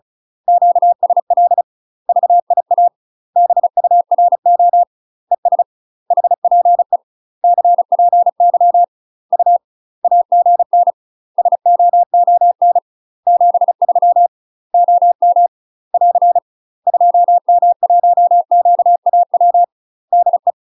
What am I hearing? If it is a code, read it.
Morse code, heard as a keyed tone, about 34 wpm, 700 Hz. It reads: QSL VIA BURO ES HPE CPY U AGN SOON 73 OM <AR> 2M1QAW DE
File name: UK MASTER QSOs TRAINING / 35WPM / UK MASTER QSOs 0043